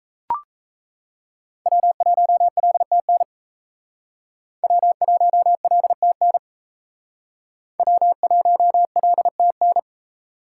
Morse code, heard as a keyed tone, about 36 wpm, 700 Hz. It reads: W1LTN W1LTN W1LTN